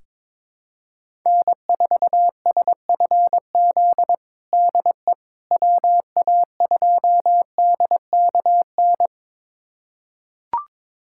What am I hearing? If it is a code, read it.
Morse code, heard as a keyed tone, about 22 wpm, 700 Hz. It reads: N4SFZ DE WA2DKN
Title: USA Callsign DE Callsign 22wpm